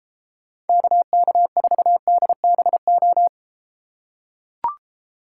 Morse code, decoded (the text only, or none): KK4DBO